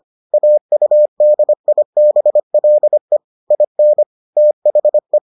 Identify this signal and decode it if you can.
Morse code, heard as a keyed tone, about 25 wpm, 600 Hz. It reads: AUDIBLE IN THE